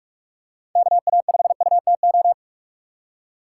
Morse code, heard as a keyed tone, about 46 wpm, 700 Hz. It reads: KA5UTO
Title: USA Call Signs - 3x Character Spacing - 45wpm